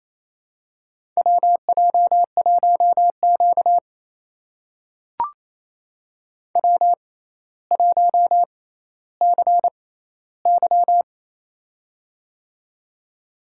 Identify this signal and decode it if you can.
Morse code, heard as a keyed tone, about 28 wpm, spaced as for 14 wpm, 700 Hz. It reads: WJ1Q W1CY
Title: USA Call Signs - 4 Characters - 6x Character Spacing - 28wpm